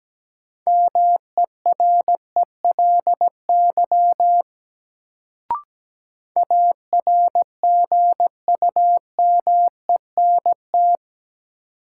Morse code, heard as a keyed tone, about 17 wpm, 700 Hz. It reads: MERELY ARGUMENT
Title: Sentences from Top 1400 Words - New Word Drill Alt - 17wpm